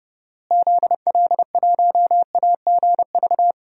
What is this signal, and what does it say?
Morse code, heard as a keyed tone, about 30 wpm, 700 Hz. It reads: ZL1AGV
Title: DXCC-100 Call Signs - 6 Characters - Repeated 3x - 30wpm